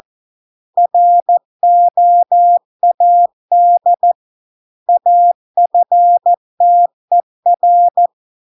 Morse code, heard as a keyed tone, about 14 wpm, 700 Hz. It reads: ROAD AFTER